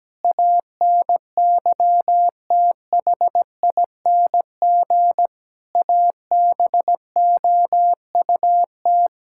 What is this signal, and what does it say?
Morse code, heard as a keyed tone, about 17 wpm, 700 Hz. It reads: ANYTHING ABOUT
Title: Sentences from Top 1600 Words 17wpm